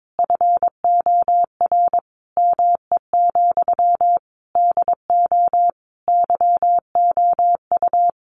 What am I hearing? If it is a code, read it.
Morse code, heard as a keyed tone, about 22 wpm, 700 Hz. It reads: FOR ME, DO YOU
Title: Sentences from Top 1500 Words 22wpm